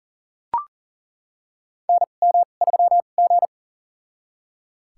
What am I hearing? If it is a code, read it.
Morse code, heard as a keyed tone, about 38 wpm, spaced as for 28 wpm, 700 Hz. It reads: NM3G
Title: USA Call Signs - 4 Characters - 2x Character Spacing - 40wpm